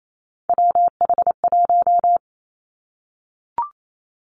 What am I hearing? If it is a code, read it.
Morse code, heard as a keyed tone, about 28 wpm, 700 Hz. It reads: WH1
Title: Three Challenging Letter-Number Combinations 28wpm